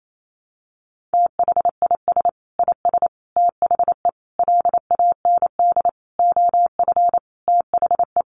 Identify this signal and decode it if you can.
Morse code, heard as a keyed tone, about 28 wpm, 700 Hz. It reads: THIS IS THE LAND OF THE
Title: Sentences from Top 1300 Words 28wpm